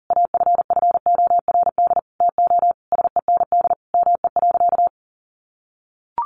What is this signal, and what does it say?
Morse code, heard as a keyed tone, about 40 wpm, 700 Hz. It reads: AFFORD TO SEND ME.
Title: Sentences from Top 1500 Words 40wpm